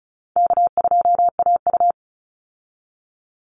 Morse code, heard as a keyed tone, about 34 wpm, 700 Hz. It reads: K2AU
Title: Easy Call Signs Repeated 3x - 35wpm